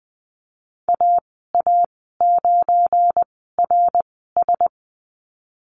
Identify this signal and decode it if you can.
Morse code, heard as a keyed tone, about 20 wpm, spaced as for 15 wpm, 700 Hz. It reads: AA9RS
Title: 186 Lesson 30g - Letter b - Call Signs 20wpm